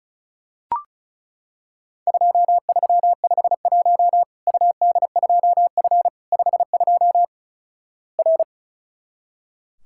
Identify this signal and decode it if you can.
Morse code, heard as a keyed tone, about 34 wpm, 700 Hz. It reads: 2351 UD2F 52 R
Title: Worked All Europe DX Contest - QTC Exchange - 35wpm